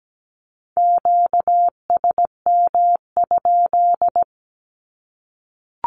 Morse code, heard as a keyed tone, about 17 wpm, 700 Hz. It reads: QSM?
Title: Q-Codes 17wpm